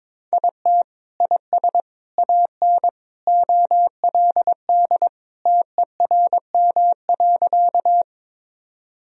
Morse code, heard as a keyed tone, about 22 wpm, 700 Hz. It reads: IT IS AN OLD TERM.
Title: Sentences from Top 1000 Words - Review Part 4 - 22wpm